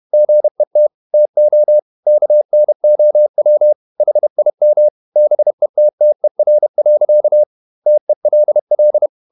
Morse code, heard as a keyed tone, about 31 wpm, 600 Hz. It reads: GET TO KNOW HIM BETTER. TELL
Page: Never 31wpm | CW med Gnister